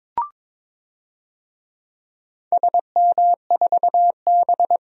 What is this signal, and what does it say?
Morse code, heard as a keyed tone, about 22 wpm, 700 Hz. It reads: SM4B
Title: DXCC-100 Call Signs - 4 Characters - 22wpm